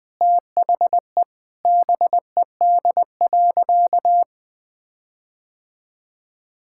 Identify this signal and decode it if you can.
Morse code, heard as a keyed tone, about 20 wpm, 700 Hz. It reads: THE BED.
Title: Sentences with 6-Word Phrases - Spoken Beginning - 20wpm